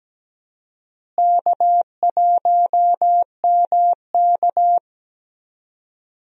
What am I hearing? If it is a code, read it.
Morse code, heard as a keyed tone, about 17 wpm, 700 Hz. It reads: K1MK
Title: USA Call Signs - 4 Characters - Repeated 3x - 17wpm